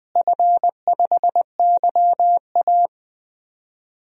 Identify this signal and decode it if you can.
Morse code, heard as a keyed tone, about 20 wpm, 700 Hz. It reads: F5YA
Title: DXCC-100 Call Signs - 4 Characters - Repeated 3x - 20wpm